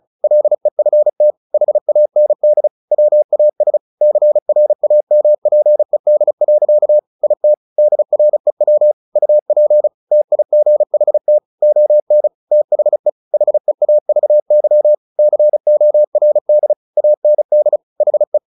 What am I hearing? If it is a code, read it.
Morse code, heard as a keyed tone, about 34 wpm, 600 Hz. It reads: LEFT HAND WAS CRAMPED. IT DREW UP TIGHT ON THE HEAVY CORD AND HE
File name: the old man and the sea / the old man and the sea 1 35wpm